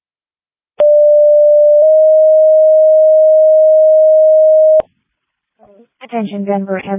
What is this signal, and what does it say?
Two-tone page, heard as a one-second tone, then a three-second tone, about 590 then 620 Hz, with speech. It is heard from 0.8 to 4.8 s.
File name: Denver Ambulance